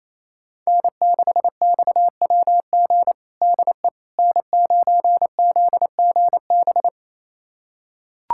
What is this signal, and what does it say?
Morse code, heard as a keyed tone, about 28 wpm, 700 Hz. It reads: N6XWG DE N9ZGB
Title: USA Callsign DE Callsign - 5 Characters - 28wpm